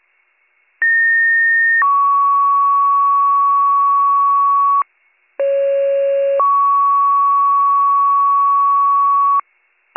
Two-tone page, heard as a one-second tone, then a three-second tone, about 1850 then 1120 Hz, about 550 then 1120 Hz. It is heard from 0.8 to 9.4 s.